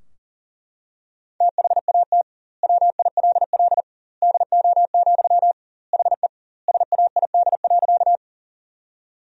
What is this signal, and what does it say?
Morse code, heard as a keyed tone, about 40 wpm, 700 Hz. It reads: THAT WILL DO, HE SAID.
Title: 2x Word-Spacing - Sentences from Top 100 Words - 40wpm